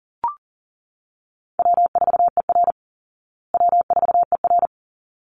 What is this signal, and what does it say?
Morse code, heard as a keyed tone, about 38 wpm, 700 Hz. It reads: W4ER W4ER
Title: Call Signs Repeated 3x - 40wpm